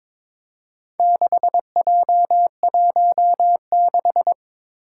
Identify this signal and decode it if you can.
Morse code, heard as a keyed tone, about 22 wpm, 700 Hz. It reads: 6J16
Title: Four Challenging Letter-Number Combinations 22wpm